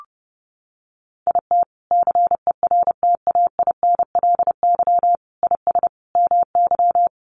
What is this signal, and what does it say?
Morse code, heard as a keyed tone, about 28 wpm, 700 Hz. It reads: IT CERTAINLY IS MY
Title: Sentences from Top 1200 Words 30wpm